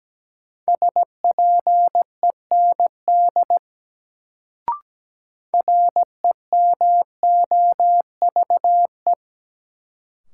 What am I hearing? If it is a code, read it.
Morse code, heard as a keyed tone, about 17 wpm, 700 Hz. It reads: SPEND REMOVE
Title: Sentences from Top 800 Words - New Word Drill - 17wpm